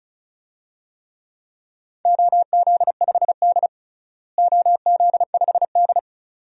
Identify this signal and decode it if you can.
Morse code, heard as a keyed tone, about 36 wpm, 700 Hz. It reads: OZ5D OZ5D
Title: DXCC-100 Call Signs - 4 Characters - Repeated 2x - 35wpm